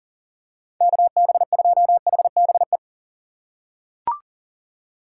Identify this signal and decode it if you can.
Morse code, heard as a keyed tone, about 40 wpm, 700 Hz. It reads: KB2HBE
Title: USA Call Signs - 6 Characters - 3x Character Spacing - 40wpm